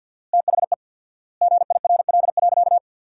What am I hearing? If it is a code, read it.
Morse code, heard as a keyed tone, about 48 wpm, 700 Hz. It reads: THE GIRL.
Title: 4x Word-Spacing - Sentences from Top 400 Words - 50wpm